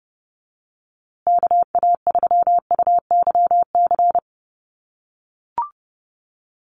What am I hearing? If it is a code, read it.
Morse code, heard as a keyed tone, about 30 wpm, 700 Hz. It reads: KA3UYC
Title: USA Call Signs Repeated 2x 30wpm